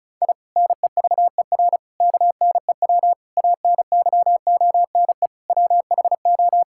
Morse code, heard as a keyed tone, about 34 wpm, 690 Hz. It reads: I NEVER KNEW ANYONE WHO
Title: Sentences from Top 1000 Words - Review Part 6 - 35wpm